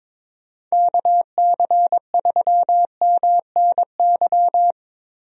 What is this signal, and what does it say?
Morse code, heard as a keyed tone, about 22 wpm, 700 Hz. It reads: KC3MNY
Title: USA Call Signs - 6 Characters - Repeated 2x - 22wpm